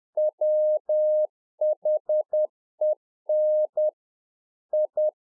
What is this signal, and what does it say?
Morse code, heard as a keyed tone, about 10 wpm, 610 Hz. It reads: WHEN I
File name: UNFOGETTABLE QSO 10WPM